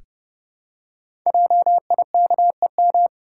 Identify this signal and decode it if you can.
Morse code, heard as a keyed tone, about 30 wpm, 700 Hz. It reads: JIKEM
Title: Code Groups - 5 Letters - 30wpm